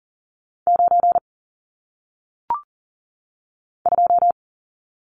Morse code, heard as a keyed tone, about 41 wpm, 700 Hz. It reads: 9 2
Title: Numbers - 1-Digit 40wpm